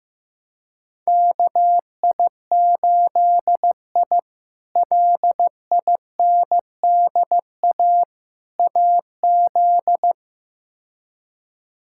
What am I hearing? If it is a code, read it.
Morse code, heard as a keyed tone, about 15 wpm, 700 Hz. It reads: KI8I LINDA AZ
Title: K1USN SST QSO Practice - 15wpm